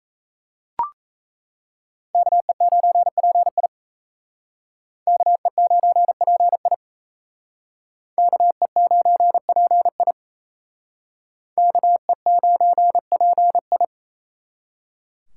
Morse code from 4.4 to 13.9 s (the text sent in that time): KE9PI KE9PI KE9PI